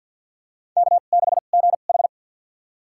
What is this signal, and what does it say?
Morse code, heard as a keyed tone, about 53 wpm, spaced as for 37 wpm, 700 Hz. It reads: K6GH
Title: USA Call Signs - 4 Characters - 2x Character Spacing - 50wpm